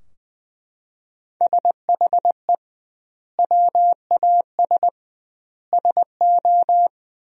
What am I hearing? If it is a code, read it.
Morse code, heard as a keyed tone, about 20 wpm, 700 Hz. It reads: SHE WAS SO
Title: 2x Word-Spacing - Sentences from Top 400 Words - 20wpm